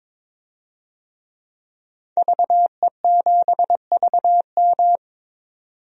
Morse code, heard as a keyed tone, about 22 wpm, 700 Hz. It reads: VE7VM
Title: DXCC-100 Call Signs - 5 Characters - 22wpm